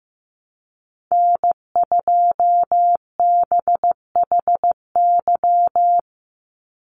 Morse code, heard as a keyed tone, about 15 wpm, 700 Hz. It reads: N2BHY
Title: USA Call Signs 15wpm